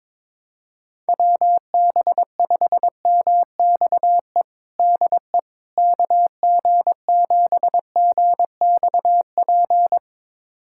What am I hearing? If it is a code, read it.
Morse code, heard as a keyed tone, about 22 wpm, 700 Hz. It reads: WB5MXE DE KG7GXP
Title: USA Callsign DE Callsign - 6 Characters - 22wpm